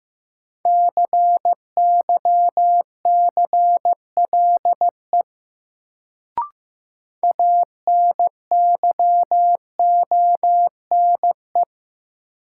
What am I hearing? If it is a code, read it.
Morse code, heard as a keyed tone, about 15 wpm, 700 Hz. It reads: CYCLE ANYONE
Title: Sentences from Top 900 Words - New Word Drill - 15wpm